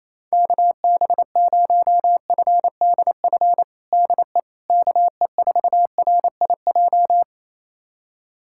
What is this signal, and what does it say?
Morse code, heard as a keyed tone, about 28 wpm, 700 Hz. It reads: KB0FDF DE KE4RIJ
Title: USA Callsign DE Callsign 28wpm